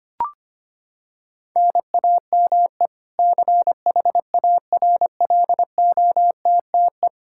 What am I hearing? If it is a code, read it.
Morse code, heard as a keyed tone, about 25 wpm, 700 Hz. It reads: NAME CHARLOTTE
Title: Practice QSO - Exchange 1 - Beginning - 25wpm